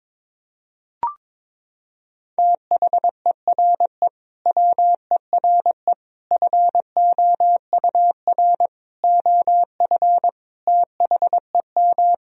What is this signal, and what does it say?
Morse code, heard as a keyed tone, about 22 wpm, 700 Hz. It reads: THERE WERE FOUR OF THEM
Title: Sentences from Top 1000 Words - Review Part 1 - 22wpm